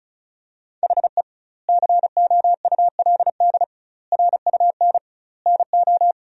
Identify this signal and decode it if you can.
Morse code, heard as a keyed tone, about 36 wpm, 700 Hz. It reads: HE COULD RUN NO
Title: 2x Word-Spacing - Sentences from Top 300 Words - 35wpm